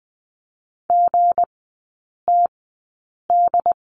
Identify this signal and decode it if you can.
Morse code, heard as a keyed tone, about 20 wpm, spaced as for 8 wpm, 700 Hz. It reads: GTD
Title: 159 Lesson 26e - Letter g - Three Characters 20wpm